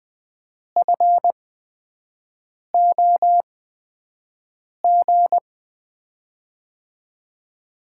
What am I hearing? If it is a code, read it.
Morse code, heard as a keyed tone, about 20 wpm, spaced as for 5 wpm, 700 Hz. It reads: FOG
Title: USA Call Signs - 5 Characters - 8x Character Spacing - 20wpm